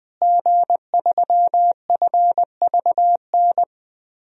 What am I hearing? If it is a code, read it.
Morse code, heard as a keyed tone, about 20 wpm, 700 Hz. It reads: G3FVN
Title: DXCC-100 Call Signs - 5 Characters - Repeated 3x - 20wpm